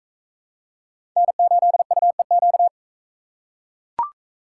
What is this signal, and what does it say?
Morse code, heard as a keyed tone, about 43 wpm, 700 Hz. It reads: N8UEQ